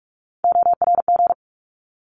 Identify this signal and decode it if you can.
Morse code, heard as a keyed tone, about 44 wpm, 700 Hz. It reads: ORG
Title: ARRL Sections 45wpm